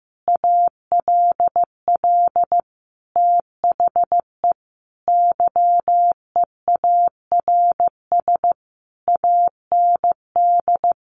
Morse code, heard as a keyed tone, about 15 wpm, 700 Hz. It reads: ALL THE YEARS AND
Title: A Sound Of Thunder 15wpm